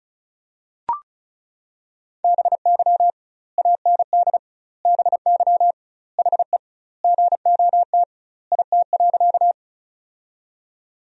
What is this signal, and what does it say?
Morse code, heard as a keyed tone, about 36 wpm, 700 Hz. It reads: BY AND BY HE GOT IT.
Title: 2x Word-Spacing - Sentences from Top 300 Words - 35wpm